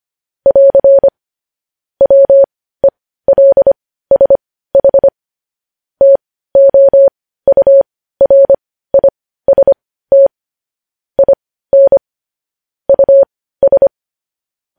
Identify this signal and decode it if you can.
Morse code, heard as a keyed tone, about 25 wpm, spaced as for 15 wpm, 550 Hz. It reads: <AR> WELSH TOURIST IN US
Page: news headlines in morse code at 15 wpm